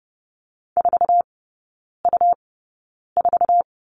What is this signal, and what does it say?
Morse code, heard as a keyed tone, about 30 wpm, spaced as for 9 wpm, 700 Hz. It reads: 4U4
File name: Lesson 17e - Letter u - Three Characters 30wpm